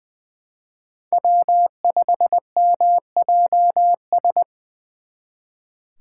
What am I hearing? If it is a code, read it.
Morse code, heard as a keyed tone, about 24 wpm, 700 Hz. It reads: W5MJS
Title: Speed Racing - USA 1x3 Call Signs 20wpm